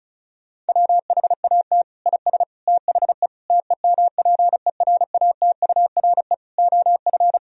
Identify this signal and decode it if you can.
Morse code, heard as a keyed tone, about 36 wpm, 700 Hz. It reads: WHAT IS THE TEMPERATURE OF
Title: Sentences from Top 1000 Words - Review Part 4 - 35wpm